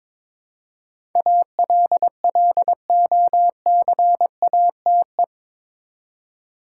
Morse code, heard as a keyed tone, about 22 wpm, 700 Hz. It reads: ALLOCATE
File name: practice / 22wpm / Eight Letter Words - 22wpm